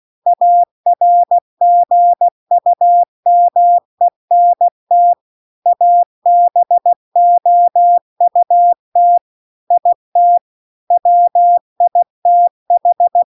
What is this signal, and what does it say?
Morse code, heard as a keyed tone, about 16 wpm, 700 Hz. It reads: ARGUMENT ABOUT IT WITH